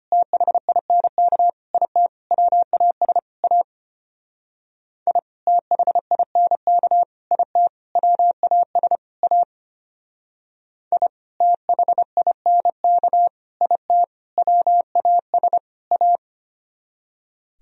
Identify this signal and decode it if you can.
Morse code, heard as a keyed tone, about 33 wpm, 700 Hz. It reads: THINK IT WAS A I THINK IT WAS A I THINK IT WAS A